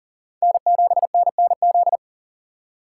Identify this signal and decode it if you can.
Morse code, heard as a keyed tone, about 37 wpm, 700 Hz. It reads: N7NNZ
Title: USA Call Signs - 2x Character Spacing - 40wpm